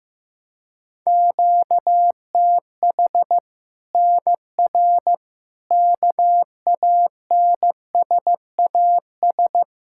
Morse code, heard as a keyed tone, about 16 wpm, 700 Hz. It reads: QTH NR KANSAS
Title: Practice QSO - Exchange 1 - Beginning - 15wpm